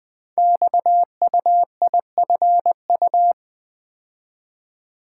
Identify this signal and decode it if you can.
Morse code, heard as a keyed tone, about 20 wpm, 700 Hz. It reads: XUIFU
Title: Code Groups - 5 Letters - 20wpm